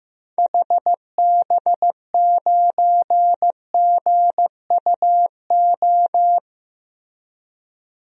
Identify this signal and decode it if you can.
Morse code, heard as a keyed tone, about 15 wpm, 700 Hz. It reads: HB9GUO
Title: DXCC-100 Call Signs - 6 Characters - Repeated 3x - 15wpm